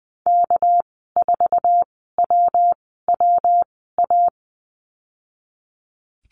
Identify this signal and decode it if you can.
Morse code, heard as a keyed tone, about 20 wpm, spaced as for 15 wpm, 700 Hz. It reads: K4WWA
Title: 231 Lesson 37g - Letter x - Call Signs 20wpm